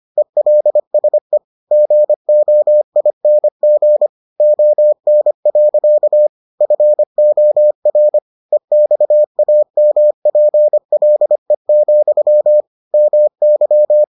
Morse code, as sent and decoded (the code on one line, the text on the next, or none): . .-.. ... / --. --- .. -. --. / --- -. .-.-.- / ..-. --- .-. / . -..- .- -- .--. .-.. --..-- / -- -.--
ELSE GOING ON. FOR EXAMPLE, MY